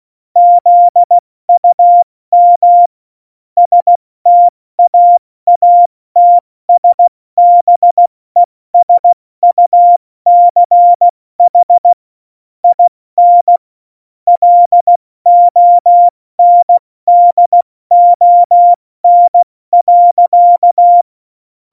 Morse code, heard as a keyed tone, about 16 wpm, 700 Hz. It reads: ZUM STAATSBESUCH IN LONDON.
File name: Morsekurs / Witze / Witz 19 Mai, 2020 70 F80